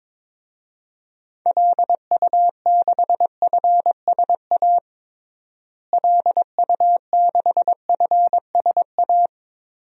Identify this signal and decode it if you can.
Morse code, heard as a keyed tone, about 22 wpm, 700 Hz. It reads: LU6FSA LU6FSA
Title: DXCC-100 Call Signs - 6 Characters - Repeated 3x - 22wpm